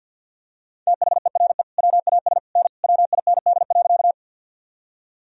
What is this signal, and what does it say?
Morse code, heard as a keyed tone, about 45 wpm, 700 Hz. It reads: THERE WAS N WIND.
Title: Sentences from Top 400 Words and Abbreviations 50wpm